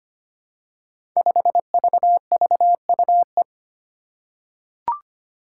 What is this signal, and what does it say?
Morse code, heard as a keyed tone, about 25 wpm, 700 Hz. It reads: 5VVUE
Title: Five Challenging Letter-Number Combinations 25wpm